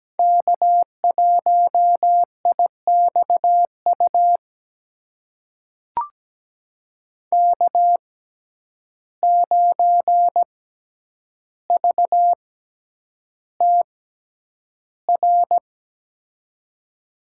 Morse code, heard as a keyed tone, about 17 wpm, spaced as for 9 wpm, 700 Hz. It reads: K1IXU K9VTR
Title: USA Call Signs - 5 Characters - 6x Character Spacing - 17wpm